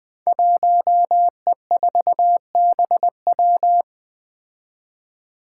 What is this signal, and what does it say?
Morse code, heard as a keyed tone, about 20 wpm, 700 Hz. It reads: 1E4BW
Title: Five Challenging Letter-Number Combinations 20wpm